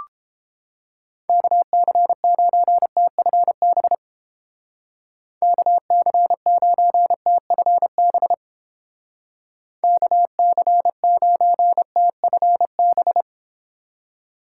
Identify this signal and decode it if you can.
Morse code, heard as a keyed tone, about 30 wpm, 700 Hz. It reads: KC9TFB KC9TFB KC9TFB